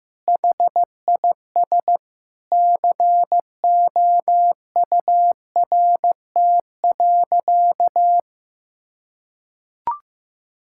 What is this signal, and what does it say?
Morse code, heard as a keyed tone, about 16 wpm, 700 Hz. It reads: HIS COURT.
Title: Sentences from Top 1600 Words 15wpm